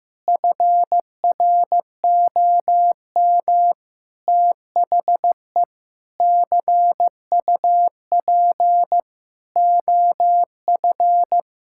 Morse code, heard as a keyed tone, about 15 wpm, 700 Hz. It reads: FROM THE CUP OF
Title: Sentences from Top 1600 Words 15wpm